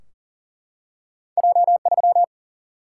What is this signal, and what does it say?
Morse code, heard as a keyed tone, about 40 wpm, 700 Hz. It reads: J 3
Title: Single-Letter Single-Number 40wpm